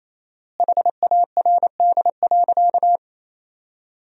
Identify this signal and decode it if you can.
Morse code, heard as a keyed tone, about 28 wpm, 700 Hz. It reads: HARD.
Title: 4x Word-Spacing - Sentences from Top 300 Words - 28wpm